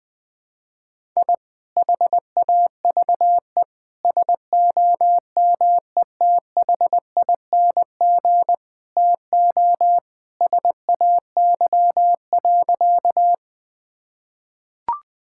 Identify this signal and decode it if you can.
Morse code, heard as a keyed tone, about 20 wpm, 700 Hz. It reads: I HAVE SOMETHING TO SAY.
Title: Sentences with 2-Word Phrases - 20wpm